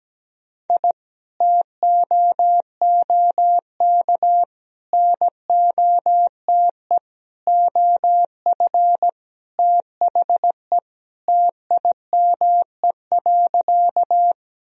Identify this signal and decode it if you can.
Morse code, heard as a keyed tone, about 17 wpm, 700 Hz. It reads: I TOOK NOTE OF THE TIME.
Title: Sentences from Top 1000 Words - Review Part 2 - 17wpm